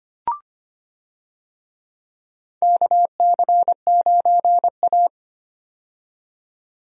Morse code, heard as a keyed tone, about 25 wpm, 700 Hz. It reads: KC9A
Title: USA Call Signs - 4 Characters - 25wpm